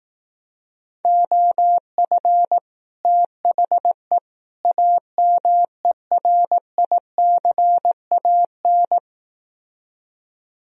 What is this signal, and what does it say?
Morse code, heard as a keyed tone, about 18 wpm, 700 Hz. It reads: OF THE AMERICAN
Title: Speed-Racing - Sets of 3 Words 15wpm